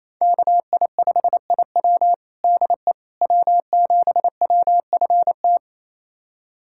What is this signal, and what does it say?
Morse code, heard as a keyed tone, about 28 wpm, 700 Hz. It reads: KI5IW DE W7WFT
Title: USA Callsign DE Callsign 28wpm